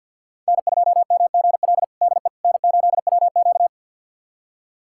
Morse code, heard as a keyed tone, about 52 wpm, 700 Hz. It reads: N2MGL DE N8WQ
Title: USA Callsign DE Callsign 50wpm